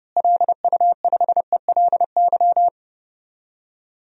Morse code, heard as a keyed tone, about 30 wpm, 700 Hz. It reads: LU5ELY
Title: DXCC-100 Call Signs - 6 Characters - Repeated 2x - 30wpm